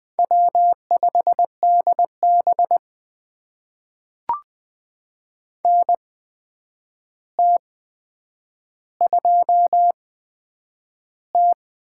Morse code, heard as a keyed tone, about 20 wpm, spaced as for 9 wpm, 700 Hz. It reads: W5DB NT2T
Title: USA Call Signs - 4 Characters - 8x Character Spacing - 20wpm